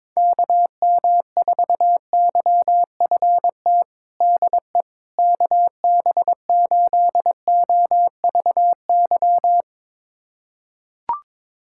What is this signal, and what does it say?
Morse code, heard as a keyed tone, about 22 wpm, 700 Hz. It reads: KM4YFT DE KB8OVY
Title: USA Callsign DE Callsign - 6 Characters - 22wpm